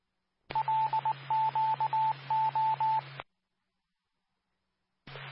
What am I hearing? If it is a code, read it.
Morse code, heard as a keyed tone, about 19 wpm, 880 Hz, with radio noise. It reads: LQO
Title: Nassau and Suffolk County Amateur Repeaters